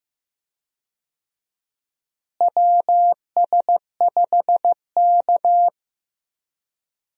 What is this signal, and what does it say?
Morse code, heard as a keyed tone, about 15 wpm, 700 Hz. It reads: WS5K
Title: USA Call Signs - 4 Characters - Repeated 2x - 15wpm